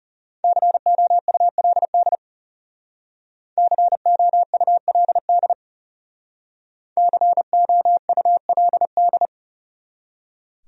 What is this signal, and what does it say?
Morse code, heard as a keyed tone, about 40 wpm, 700 Hz. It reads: COULD COULD COULD